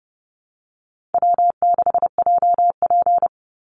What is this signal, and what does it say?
Morse code, heard as a keyed tone, about 30 wpm, 700 Hz. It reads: W6JP
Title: NA SOTA Chaser Callsigns 30wpm